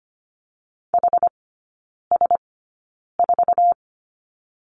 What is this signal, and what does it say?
Morse code, heard as a keyed tone, about 25 wpm, spaced as for 9 wpm, 700 Hz. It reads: HS4
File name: Lesson 17e - Letter u - Three Characters 25wpm